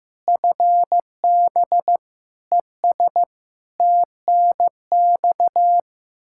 Morse code, heard as a keyed tone, about 16 wpm, 700 Hz. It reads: FB ES TNX
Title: Practice QSO - Exchange 2 - Info - 15wpm